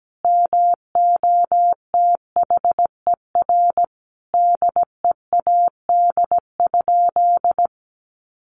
Morse code, heard as a keyed tone, about 17 wpm, 700 Hz. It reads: MOTHER DEAD?
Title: Sentences from Top 1500 Words 17wpm